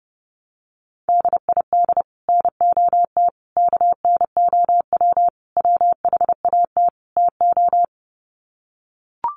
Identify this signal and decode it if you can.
Morse code, heard as a keyed tone, about 30 wpm, 700 Hz. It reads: DID NOT KNOW WHAT TO
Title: Sets of 5 Words 30wpm